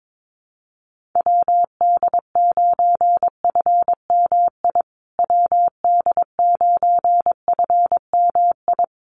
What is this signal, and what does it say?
Morse code, heard as a keyed tone, about 22 wpm, 700 Hz. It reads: WD9FMI WD9FMI
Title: USA Call Signs Repeated 3x 22wpm